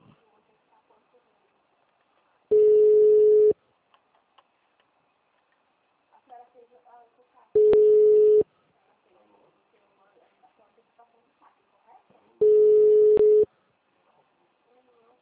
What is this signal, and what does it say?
Ringback tone (Europe, 425 Hz), with speech.